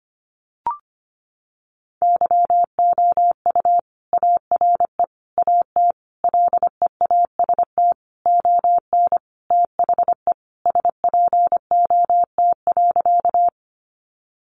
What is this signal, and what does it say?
Morse code, heard as a keyed tone, about 25 wpm, 700 Hz. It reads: YOU ARE AT LEAST ON THE SPOT.
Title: Sentences from Top 1100 Words 25wpm